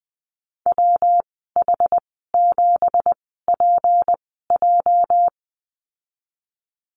Morse code, heard as a keyed tone, about 20 wpm, spaced as for 14 wpm, 700 Hz. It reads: WH7PJ
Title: 219 Lesson 35g - Number 0 - Call Signs 20wpm